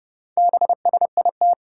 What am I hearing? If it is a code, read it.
Morse code, heard as a keyed tone, about 30 wpm, 700 Hz. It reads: BSIT
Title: Four Challenging Letter-Number Combinations 30wpm